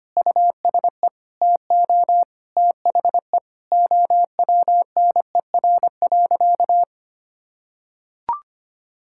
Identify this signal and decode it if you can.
Morse code, heard as a keyed tone, about 25 wpm, 700 Hz. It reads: USE TO THE OWNER.
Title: Sentences from Top 1600 Words 25wpm